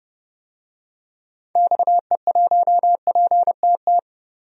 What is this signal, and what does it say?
Morse code, heard as a keyed tone, about 30 wpm, 700 Hz. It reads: XE1PTT
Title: DXCC-100 Call Signs - 6 Characters - Repeated 3x - 30wpm